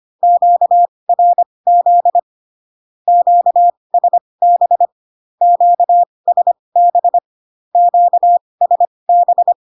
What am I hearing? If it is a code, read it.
Morse code, heard as a keyed tone, about 25 wpm, spaced as for 20 wpm, 700 Hz. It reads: QRZ QSB QSB QSB
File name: cw / Q-codes most common repeated 3 times 15 25wpm ik1hge